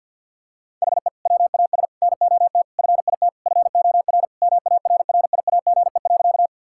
Morse code, heard as a keyed tone, about 50 wpm, 700 Hz. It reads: HE WAS NOT FIT FOR MARRIAGE.
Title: Sentences from Top 1000 Words - Review Part 5 - 50wpm